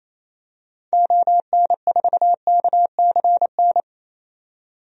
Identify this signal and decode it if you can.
Morse code, heard as a keyed tone, about 28 wpm, 700 Hz. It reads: ON4KCN
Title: DXCC-100 Call Signs - 6 Characters - 28wpm